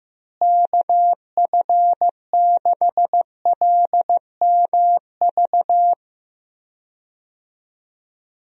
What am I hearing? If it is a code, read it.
Morse code, heard as a keyed tone, about 15 wpm, 700 Hz. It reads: KF6LMV
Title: USA Call Signs - 6 Characters - Repeated 3x - 15wpm